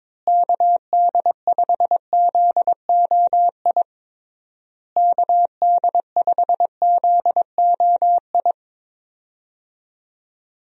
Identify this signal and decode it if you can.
Morse code, heard as a keyed tone, about 22 wpm, 700 Hz. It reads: KD5ZOI KD5ZOI
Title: USA Call Signs - 6 Characters - Repeated 3x - 22wpm